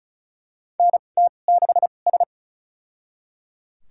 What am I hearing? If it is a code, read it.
Morse code, heard as a keyed tone, about 34 wpm, spaced as for 25 wpm, 700 Hz. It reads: NT6S
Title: USA Call Signs - 4 Characters - 2x Character Spacing - 35wpm